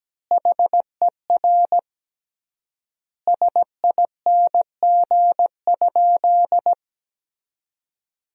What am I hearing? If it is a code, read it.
Morse code, heard as a keyed tone, about 17 wpm, 700 Hz. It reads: HER SING?
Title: 3x Word-Spacing - Sentences from Top 400 Words - 17wpm